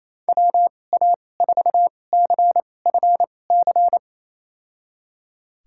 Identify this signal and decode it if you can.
Morse code, heard as a keyed tone, about 28 wpm, spaced as for 20 wpm, 700 Hz. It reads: WA4CFC
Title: USA Call Signs - 6 Characters - 2x Character Spacing - 28wpm